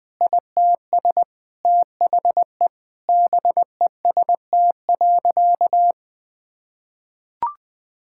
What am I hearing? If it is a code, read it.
Morse code, heard as a keyed tone, about 20 wpm, 700 Hz. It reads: ITS THE BEST.
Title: Sentences with 3-Word Phrases - 20wpm